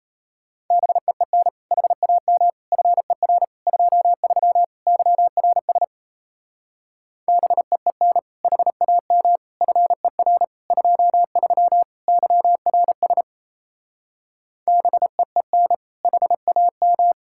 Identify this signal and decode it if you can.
Morse code, heard as a keyed tone, about 34 wpm, 700 Hz. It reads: BEEN HAM FER 23 YRS BEEN HAM FER 23 YRS BEEN HAM
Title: Speed-Racing Common QSO Phrases Encore - 28wpm